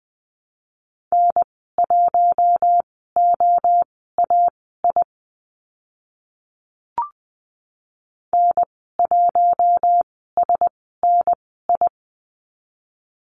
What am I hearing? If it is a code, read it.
Morse code, heard as a keyed tone, about 20 wpm, spaced as for 14 wpm, 700 Hz. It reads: N1OAI N1SNI
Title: 042 Lesson 8g - Number 1 - Call Signs 20wpm